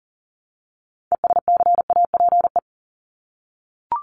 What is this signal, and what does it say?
Morse code, heard as a keyed tone, about 40 wpm, 700 Hz. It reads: ESCAPE
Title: Top 1501-1750 Words 40wpm